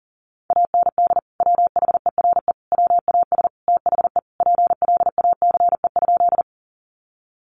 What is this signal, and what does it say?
Morse code, heard as a keyed tone, about 40 wpm, 700 Hz. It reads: AND WHERE WAS THE PLACE?
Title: Sentences from Top 200 Words 40wpm